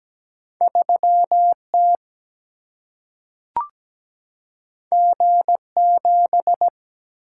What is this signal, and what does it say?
Morse code, heard as a keyed tone, about 17 wpm, 700 Hz. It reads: 3 T G 7
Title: Single-Letter Single-Number 17wpm